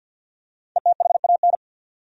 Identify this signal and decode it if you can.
Morse code, heard as a keyed tone, about 50 wpm, 700 Hz. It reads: ETHAN
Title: USA Common Names - Max 5 Letters - 50wpm